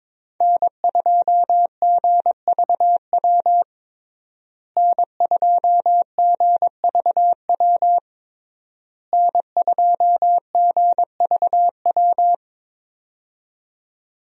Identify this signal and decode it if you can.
Morse code, heard as a keyed tone, about 22 wpm, 700 Hz. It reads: N2GVW N2GVW N2GVW
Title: USA Call Signs - 5 Characters - Repeated 3x - 22wpm